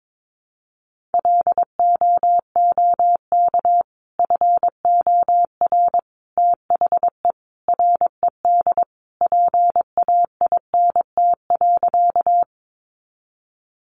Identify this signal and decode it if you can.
Morse code, heard as a keyed tone, about 22 wpm, 700 Hz. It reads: LOOK FOR THE RED PAINT.
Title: A Sound Of Thunder 22wpm